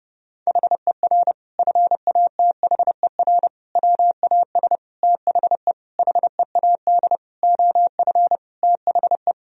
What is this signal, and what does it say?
Morse code, heard as a keyed tone, about 32 wpm, 700 Hz. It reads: HER FATHER WAS THE HEAD OF THE
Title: Sentences from Top 1000 Words - Review Part 4 - 30wpm